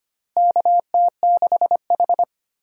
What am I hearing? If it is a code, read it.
Morse code, heard as a keyed tone, about 25 wpm, 700 Hz. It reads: KT6H
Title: USA Call Signs - 4 Characters - Repeated 2x - 25wpm